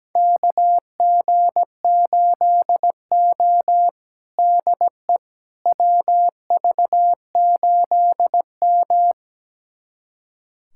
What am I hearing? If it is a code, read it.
Morse code, heard as a keyed tone, about 17 wpm, 700 Hz. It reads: KG8O DE WV8M
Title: USA Callsign DE Callsign - 4 Characters - 17wpm